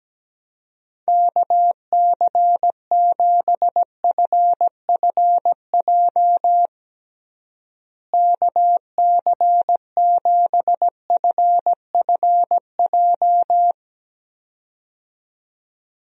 Morse code, heard as a keyed tone, about 17 wpm, 700 Hz. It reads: KC7FFJ KC7FFJ
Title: USA Call Signs - 6 Characters - Repeated 2x - 17wpm